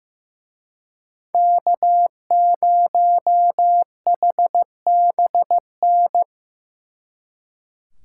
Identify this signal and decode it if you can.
Morse code, heard as a keyed tone, about 17 wpm, 700 Hz. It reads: K0HBN
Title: Speed Racing - USA 1x3 Call Signs 15wpm